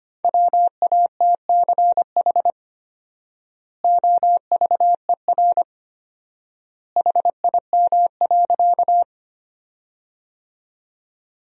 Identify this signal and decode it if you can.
Morse code, heard as a keyed tone, about 25 wpm, 700 Hz. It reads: WATCH OVER HIM.
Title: 4x Word-Spacing - Sentences from Top 300 Words - 25wpm